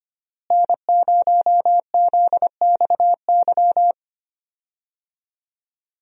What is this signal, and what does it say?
Morse code, heard as a keyed tone, about 25 wpm, 700 Hz. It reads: N0ZXY
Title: USA Call Signs - 5 Characters - Repeated 3x - 25wpm